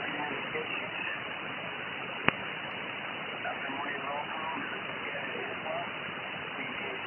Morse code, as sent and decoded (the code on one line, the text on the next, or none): none